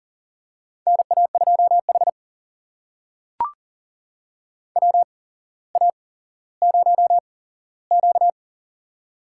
Morse code, heard as a keyed tone, about 40 wpm, spaced as for 17 wpm, 700 Hz. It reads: NA2H WA0Q
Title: USA Call Signs - 4 Characters - 8x Character Spacing - 40wpm